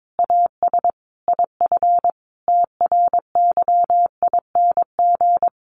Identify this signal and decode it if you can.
Morse code, heard as a keyed tone, about 22 wpm, 700 Hz. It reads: AS IF TRYING
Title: A Sound Of Thunder 22wpm